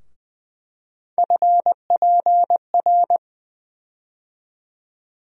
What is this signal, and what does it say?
Morse code, heard as a keyed tone, about 20 wpm, 700 Hz. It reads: FPR
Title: USA Call Sign Postfixes - 3 Letters 20wpm